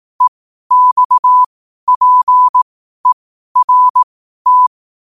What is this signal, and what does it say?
Morse code, heard as a keyed tone, about 18 wpm, spaced as for 12 wpm, 1000 Hz. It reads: EXPERT
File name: qotdfiles / 12wpm